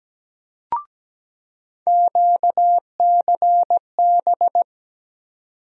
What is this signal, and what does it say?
Morse code, heard as a keyed tone, about 17 wpm, 700 Hz. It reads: QCB
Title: USA Call Sign Postfixes - 3 Letters 17wpm